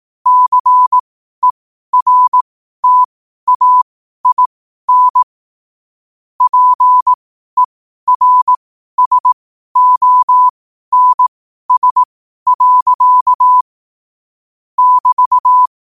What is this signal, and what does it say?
Morse code, heard as a keyed tone, about 18 wpm, spaced as for 12 wpm, 1000 Hz. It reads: CERTAIN PERSONS. <BT>
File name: qotdfiles / 12wpm